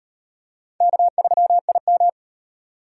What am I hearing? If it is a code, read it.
Morse code, heard as a keyed tone, about 33 wpm, 700 Hz. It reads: K3IM